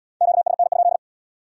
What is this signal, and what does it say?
Morse code, heard as a keyed tone, about 66 wpm, 700 Hz. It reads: QSN?